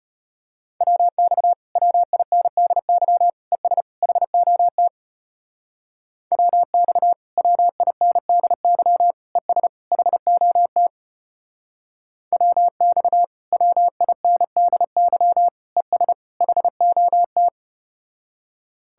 Morse code, heard as a keyed tone, about 33 wpm, 700 Hz. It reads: WX WINDY ES HOT WX WINDY ES HOT WX WINDY ES HOT